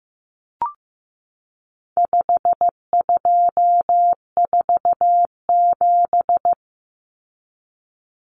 Morse code, heard as a keyed tone, about 15 wpm, 700 Hz. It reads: 5247
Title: Common Cut Numbers - 4-Digit 15wpm